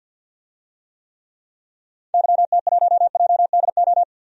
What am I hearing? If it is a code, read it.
Morse code, heard as a keyed tone, about 52 wpm, 700 Hz. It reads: YT1JDO
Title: DXCC-100 Call Signs - 6 Characters - 50wpm